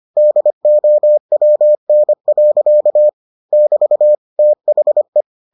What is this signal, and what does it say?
Morse code, heard as a keyed tone, about 25 wpm, 600 Hz. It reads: DOWN. <BT> THE